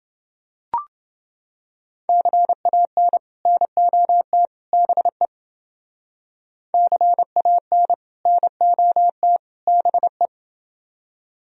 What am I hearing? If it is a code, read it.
Morse code, heard as a keyed tone, about 29 wpm, 700 Hz. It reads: CAN NOT BE CAN NOT BE